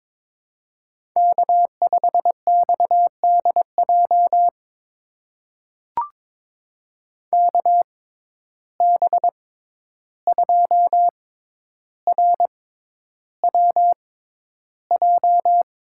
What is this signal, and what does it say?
Morse code, heard as a keyed tone, about 22 wpm, spaced as for 11 wpm, 700 Hz. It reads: K5XDJ KB2RWJ
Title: USA Call Signs - 6x Character Spacing - 22wpm